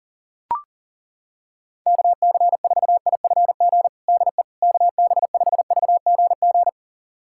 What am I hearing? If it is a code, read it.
Morse code, heard as a keyed tone, about 40 wpm, 700 Hz. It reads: KC4IFG DE KB5VGG
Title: USA Callsign DE Callsign - 6 Characters - 40wpm